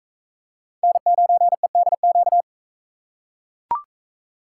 Morse code, heard as a keyed tone, about 41 wpm, 700 Hz. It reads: N9EDQ